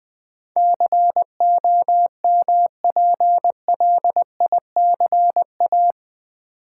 Morse code, heard as a keyed tone, about 20 wpm, 700 Hz. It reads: COMPLICA
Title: Build A Word - 4 Letters At A Time - 12-Letter Words - 20wpm